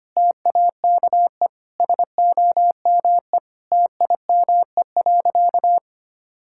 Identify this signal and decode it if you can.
Morse code, heard as a keyed tone, about 25 wpm, 700 Hz. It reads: TAKE SOME TIME.
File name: practice / 25wpm / Sentences with 3-Word Phrases - 25wpm